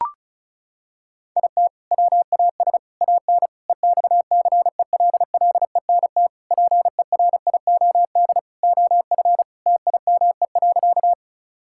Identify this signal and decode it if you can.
Morse code, heard as a keyed tone, about 34 wpm, 700 Hz. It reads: IT WAS AN EXCELLENT PERIOD OF TIME.
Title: Sentences from Top 1000 Words - Review Part 5 - 35wpm